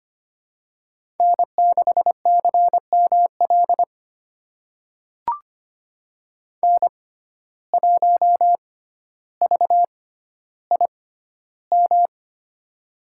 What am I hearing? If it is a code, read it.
Morse code, heard as a keyed tone, about 25 wpm, spaced as for 13 wpm, 700 Hz. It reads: N6CML N1VIM
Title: USA Call Signs - 5 Characters - 6x Character Spacing - 25wpm